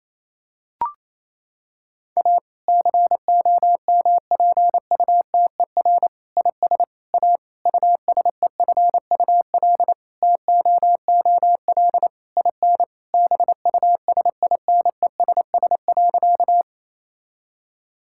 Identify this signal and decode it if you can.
Morse code, heard as a keyed tone, about 28 wpm, 700 Hz. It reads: A COMPUTER IS A USEFUL TOOL IN BUSINESS.
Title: Sentences from Top 1000 Words - Review Part 6 - 28wpm